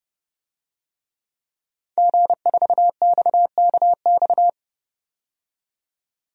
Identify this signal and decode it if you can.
Morse code, heard as a keyed tone, about 32 wpm, 700 Hz. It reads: G4XKX
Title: DXCC-100 Call Signs - 5 Characters - 30wpm